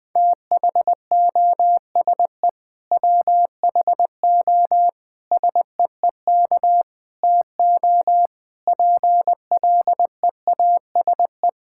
Morse code, heard as a keyed tone, about 20 wpm, 700 Hz. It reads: THOSE WHO SEEK TO PLEASE
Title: Sentences from Top 1600 Words 20wpm